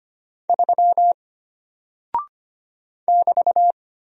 Morse code, heard as a keyed tone, about 25 wpm, 700 Hz. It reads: 3 <BT>
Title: Single Character (Letter:Number:Punctuation:Prosign) Alt 25wpm